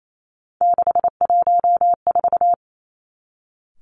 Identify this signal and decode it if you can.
Morse code, heard as a keyed tone, about 28 wpm, 690 Hz. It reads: 614
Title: Three Challenging Letter-Number Combinations 28wpm